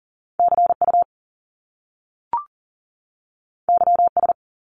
Morse code, heard as a keyed tone, about 40 wpm, 700 Hz. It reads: CU YS
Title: Top Two Letter Combinations in Words 40wpm